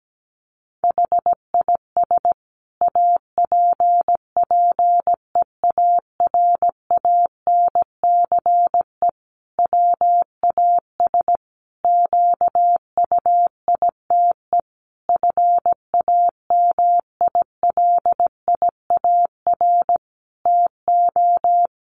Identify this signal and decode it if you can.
Morse code, heard as a keyed tone, about 17 wpm, 700 Hz. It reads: HIS APPEARANCE WAS QUITE FAMILIAR TO
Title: Sentences from Top 1100 Words 17wpm